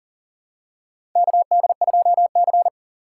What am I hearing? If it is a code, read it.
Morse code, heard as a keyed tone, about 40 wpm, 700 Hz. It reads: KD2C
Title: USA Call Signs - 3x Character Spacing - 40wpm